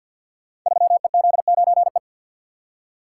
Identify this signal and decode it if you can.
Morse code, heard as a keyed tone, about 53 wpm, 700 Hz. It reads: 3EZ9E
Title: Code Groups - 5 Letter:Numbers - 50wpm